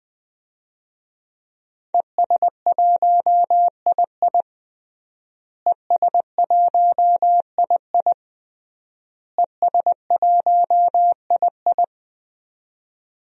Morse code, heard as a keyed tone, about 20 wpm, 700 Hz. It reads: ES1II ES1II ES1II
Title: DXCC-100 Call Signs - 5 Characters - Repeated 3x - 20wpm